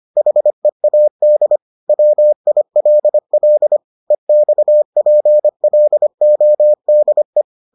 Morse code, heard as a keyed tone, about 25 wpm, 600 Hz. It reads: HEAD WILL EXPLODE